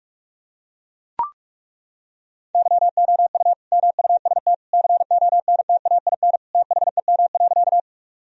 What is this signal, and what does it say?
Morse code, heard as a keyed tone, about 41 wpm, 700 Hz. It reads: YOU MUST CONTAIN THEM.
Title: Sentences from Top 1000 Words - Review Part 2 - 45wpm